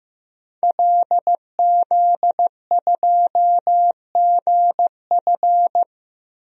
Morse code, heard as a keyed tone, about 15 wpm, 700 Hz. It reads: LZ2GF
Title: DXCC-100 Call Signs - 5 Characters - 15wpm